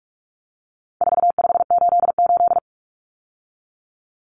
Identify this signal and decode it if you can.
Morse code, heard as a keyed tone, about 44 wpm, 700 Hz. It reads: 4588
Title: Common Cut Numbers - 4-Digit 45wpm